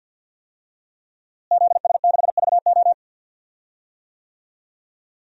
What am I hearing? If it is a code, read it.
Morse code, heard as a keyed tone, about 47 wpm, 700 Hz. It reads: ZS6VO
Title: DXCC-100 Call Signs - 5 Characters - Repeated 2x - 50wpm